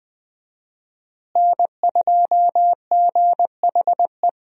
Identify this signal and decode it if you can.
Morse code, heard as a keyed tone, about 20 wpm, 700 Hz. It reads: N2GHE
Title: USA Call Signs - 5 Characters - Repeated 3x - 20wpm